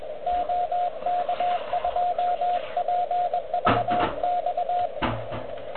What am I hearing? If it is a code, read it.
Morse code, heard as a keyed tone, about 20 wpm, 710 Hz, with radio noise. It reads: OK2PVX
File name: OK QRP 2022 / Zvuk / REC38